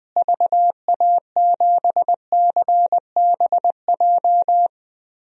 Morse code, heard as a keyed tone, about 20 wpm, 700 Hz. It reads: VA7CBJ
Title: DXCC-100 Call Signs - 6 Characters - Repeated 3x - 20wpm